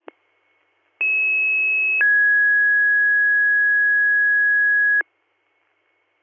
Two-tone page, heard as a one-second tone, then a three-second tone, about 2600 then 1650 Hz. It is heard from 1.0 to 5.0 s.